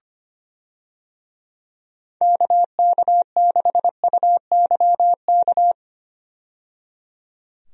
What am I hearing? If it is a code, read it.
Morse code, heard as a keyed tone, about 25 wpm, 700 Hz. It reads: KK6UYK
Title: USA Call Signs - 6 Characters - 25wpm